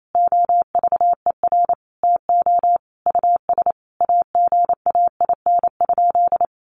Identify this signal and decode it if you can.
Morse code, heard as a keyed tone, about 28 wpm, 700 Hz. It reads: OVER TO US AGAIN?
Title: Sentences from Top 1200 Words 28wpm